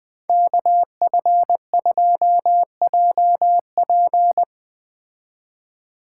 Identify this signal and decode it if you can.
Morse code, heard as a keyed tone, about 20 wpm, 700 Hz. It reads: KF2JP